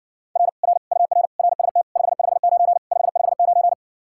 Morse code, heard as a keyed tone, about 60 wpm, 700 Hz. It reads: R R UR RST 559 559